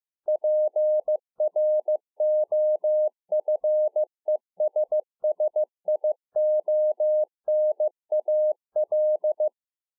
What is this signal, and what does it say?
Morse code, heard as a keyed tone, about 15 wpm, 610 Hz. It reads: PROFESSIONAL